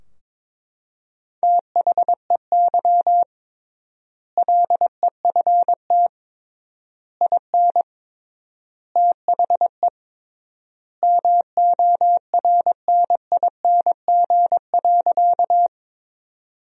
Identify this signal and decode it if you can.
Morse code, heard as a keyed tone, about 21 wpm, 700 Hz. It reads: THEY LEFT IN THE MORNING.
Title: 3x Word-Spacing - Sentences from Top 400 Words - 22wpm